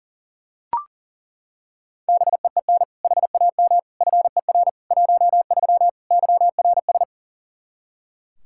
Morse code, heard as a keyed tone, about 37 wpm, 700 Hz. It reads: BEEN HAM FER 13 YRS
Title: Practice QSO - Exchange 3 - More Info - 40wpm